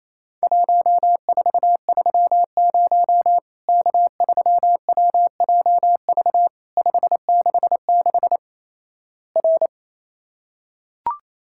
Morse code, heard as a keyed tone, about 28 wpm, 700 Hz. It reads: 1430 K3WJV 566 R
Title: Worked All Europe DX Contest - QTC Exchange - 28wpm